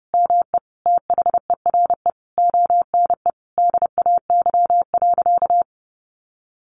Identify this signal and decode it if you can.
Morse code, heard as a keyed tone, about 30 wpm, 700 Hz. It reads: ME THERE ONE DAY.
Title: Sentences from Top 300 Words 30wpm